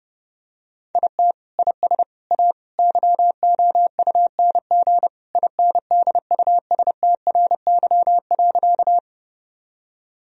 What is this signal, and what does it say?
Morse code, heard as a keyed tone, about 32 wpm, 700 Hz. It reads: IT IS A YOUNG INDUSTRY.
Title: Sentences from Top 1000 Words - Review Part 4 - 30wpm